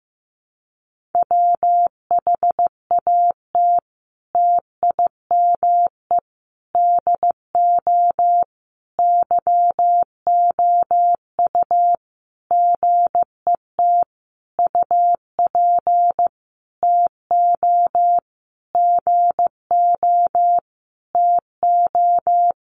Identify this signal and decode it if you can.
Morse code, heard as a keyed tone, about 15 wpm, 700 Hz. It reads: WHAT TIME DO YOU GET UP TO GO TO
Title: Sentences from Top 1500 Words 15wpm